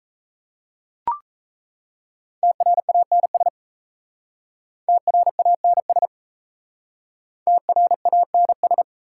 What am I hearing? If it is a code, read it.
Morse code, heard as a keyed tone, about 37 wpm, 700 Hz. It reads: TRANS TRANS TRANS